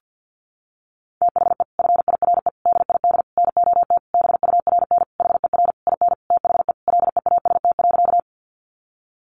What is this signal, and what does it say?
Morse code, heard as a keyed tone, about 50 wpm, 700 Hz. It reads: THE FIRE DID NOT BURN HER IN THE LEAST.
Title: Sentences from Top 1300 Words 50wpm